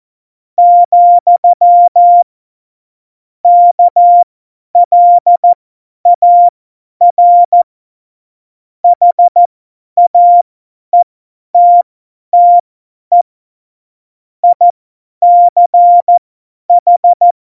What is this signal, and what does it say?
Morse code, heard as a keyed tone, about 14 wpm, spaced as for 10 wpm, 700 Hz. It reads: , KLAR HAETTE ICH